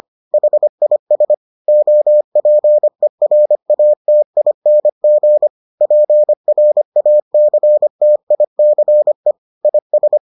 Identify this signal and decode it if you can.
Morse code, heard as a keyed tone, about 25 wpm, 600 Hz. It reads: HIS OPERATING PRACTICE IS